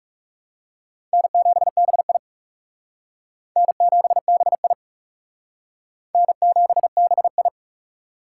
Morse code, heard as a keyed tone, about 36 wpm, 700 Hz. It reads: N7BI N7BI N7BI
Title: Speed Racing - USA 1x2 Call Signs 30wpm